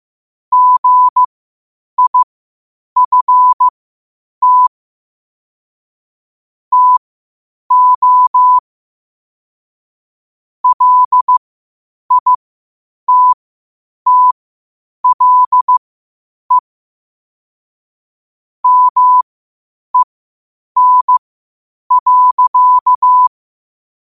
Morse code, heard as a keyed tone, about 15 wpm, spaced as for 8 wpm, 1000 Hz. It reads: GIFT TO LITTLE MEN.
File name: qotdfiles / 8wpm